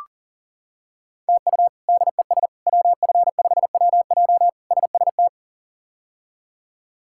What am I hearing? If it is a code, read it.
Morse code, heard as a keyed tone, about 40 wpm, 700 Hz. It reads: TU DES WF5WJ SST
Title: K1USN SST QSO Practice - 40wpm